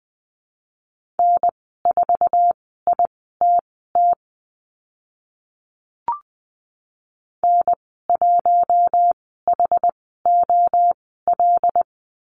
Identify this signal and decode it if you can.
Morse code, heard as a keyed tone, about 20 wpm, spaced as for 14 wpm, 700 Hz. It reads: N4ITT N1HOL
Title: 082 Lesson 14g - Number 2 - Call Signs 20wpm